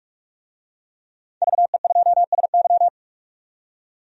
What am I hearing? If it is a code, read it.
Morse code, heard as a keyed tone, about 42 wpm, 700 Hz. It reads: VE2SY
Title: DXCC-100 Call Signs - 5 Characters - 45wpm